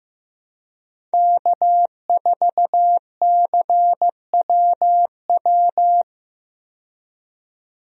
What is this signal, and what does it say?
Morse code, heard as a keyed tone, about 15 wpm, 700 Hz. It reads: K4CWW
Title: USA Call Signs - 5 Characters - 15wpm